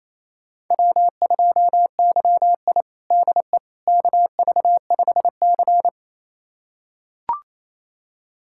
Morse code, heard as a keyed tone, about 28 wpm, 700 Hz. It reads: W2YI DE KV5C
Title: USA Callsign DE Callsign - 4 Characters - 28wpm